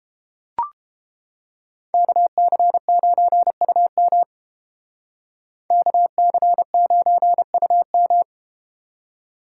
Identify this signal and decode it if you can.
Morse code, heard as a keyed tone, about 31 wpm, 700 Hz. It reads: KC9UM KC9UM